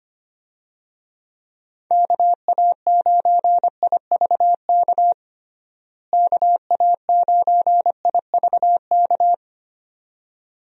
Morse code, heard as a keyed tone, about 25 wpm, 700 Hz. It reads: KA9IVK KA9IVK
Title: USA Call Signs - 6 Characters - Repeated 2x - 25wpm